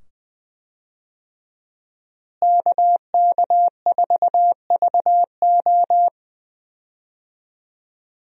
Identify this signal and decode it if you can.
Morse code, heard as a keyed tone, about 20 wpm, 700 Hz. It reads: KK4VO
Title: USA Call Signs - 5 Characters - Repeated 2x - 20wpm